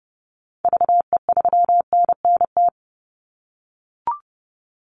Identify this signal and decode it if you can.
Morse code, heard as a keyed tone, about 30 wpm, 700 Hz. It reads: VE3NNT
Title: Call Signs - 30wpm